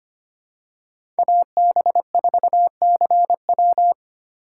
Morse code, heard as a keyed tone, about 25 wpm, 700 Hz. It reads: AB4CW
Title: USA Call Signs - 5 Characters - 8x Character Spacing - 25wpm